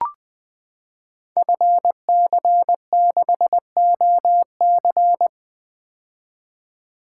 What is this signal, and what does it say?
Morse code, heard as a keyed tone, about 21 wpm, 700 Hz. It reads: FC6OC
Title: Code Groups - 5 Letter:Numbers - 20wpm